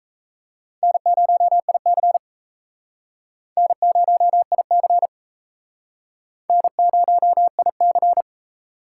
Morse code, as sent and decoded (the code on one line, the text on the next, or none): -. ----- .. -.-. / -. ----- .. -.-. / -. ----- .. -.-.
N0IC N0IC N0IC